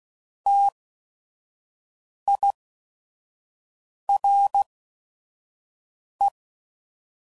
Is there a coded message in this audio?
Yes: Morse code from 0.5 to 6.3 s, heard as a keyed tone, about 16 wpm, spaced as for 5 wpm, 800 Hz. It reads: TIRE